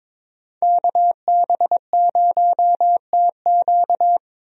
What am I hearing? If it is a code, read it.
Morse code, heard as a keyed tone, about 22 wpm, 700 Hz. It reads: KB0TQ
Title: USA Call Signs - 5 Characters - Repeated 3x - 22wpm